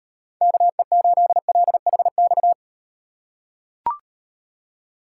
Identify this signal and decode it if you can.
Morse code, heard as a keyed tone, about 38 wpm, 700 Hz. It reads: KE8LHX